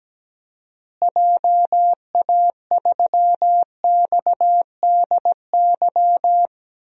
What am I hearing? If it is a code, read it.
Morse code, heard as a keyed tone, about 17 wpm, 700 Hz. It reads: JA3XDY
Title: DXCC-100 Call Signs - 6 Characters - Repeated 2x - 17wpm